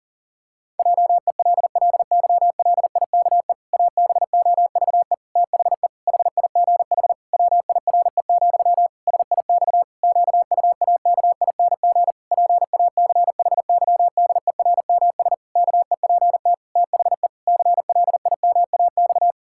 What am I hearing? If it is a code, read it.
Morse code, heard as a keyed tone, about 40 wpm, 700 Hz. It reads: JELLYLIKE ABOVE THE HIGH WIRE, SIX QUAKING PACHYDERMS KEPT THE CLIMAX
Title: Pangram Challenge v2 - 40wpm